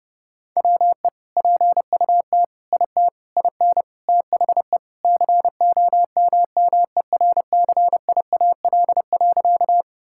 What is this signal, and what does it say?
Morse code, heard as a keyed tone, about 30 wpm, 700 Hz. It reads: WE PUT IT IN THE COMMERCIAL.
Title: Sentences from Top 1000 Words - Review Part 5 - 30wpm